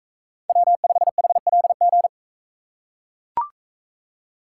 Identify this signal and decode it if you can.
Morse code, heard as a keyed tone, about 42 wpm, 700 Hz. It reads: W5HLG